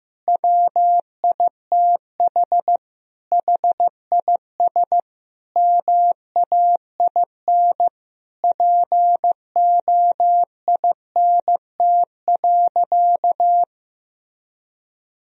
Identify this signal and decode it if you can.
Morse code, heard as a keyed tone, about 15 wpm, 700 Hz. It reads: WITH HIS MAIN POINT.
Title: Sentences from Top 1000 Words - Review Part 2 - 15wpm